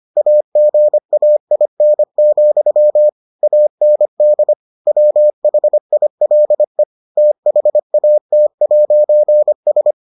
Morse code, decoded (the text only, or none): AGAIN, AND WHILE THAT'S